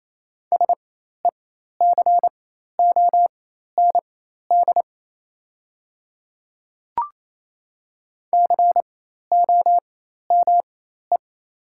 Morse code, heard as a keyed tone, about 28 wpm, spaced as for 13 wpm, 700 Hz. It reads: SECOND COME
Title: 4x Character Spacing - Top 300 Words - 28wpm